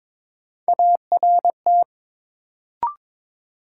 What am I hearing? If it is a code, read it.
Morse code, heard as a keyed tone, about 22 wpm, 700 Hz. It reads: ART
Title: Sentences from Top 600 Words - New Word Drill - 22wpm